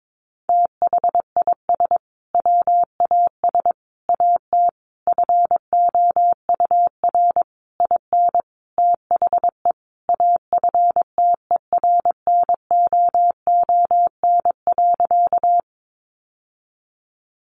Morse code, heard as a keyed tone, about 22 wpm, 700 Hz. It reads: THIS WAS AT FOUR IN THE AFTERNOON.
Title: Sentences from Top 1500 Words 22wpm